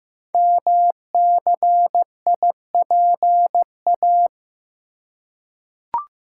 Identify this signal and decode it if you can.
Morse code, heard as a keyed tone, about 15 wpm, 700 Hz. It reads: MCIPA
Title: Code Groups - 5 Letters - 15wpm